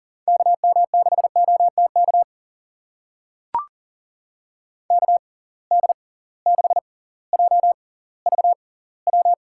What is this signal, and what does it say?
Morse code, heard as a keyed tone, about 40 wpm, spaced as for 20 wpm, 700 Hz. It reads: KM6OTK KD6JVW
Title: USA Call Signs - 6 Characters - 6x Character Spacing - 40wpm